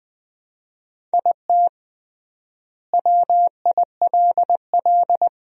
Morse code, heard as a keyed tone, about 20 wpm, 700 Hz. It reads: IT WILL
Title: 3x Word-Spacing - Sentences from Top 400 Words - 20wpm